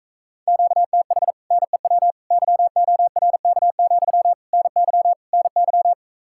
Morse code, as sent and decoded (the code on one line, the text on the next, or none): --.- - .... / -. .-- / -.-- --- .-. -.- --..-- / -. -.-- / -. -.--
QTH NEW YORK, NY NY